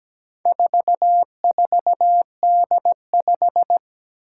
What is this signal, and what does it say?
Morse code, heard as a keyed tone, about 17 wpm, 700 Hz. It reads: 44D5
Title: Four Challenging Letter-Number Combinations 17wpm